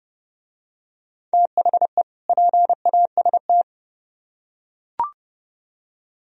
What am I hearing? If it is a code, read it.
Morse code, heard as a keyed tone, about 30 wpm, 700 Hz. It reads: THE PAST